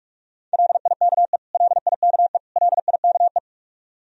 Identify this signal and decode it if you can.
Morse code, heard as a keyed tone, about 46 wpm, 700 Hz. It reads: LIKE LIKE LIKE
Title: Top 100 Words Repeated 45wpm